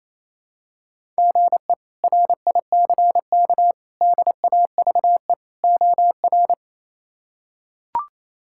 Morse code, heard as a keyed tone, about 28 wpm, 700 Hz. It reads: GE RICK DAVE OR
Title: K1USN SST QSO Practice - 28wpm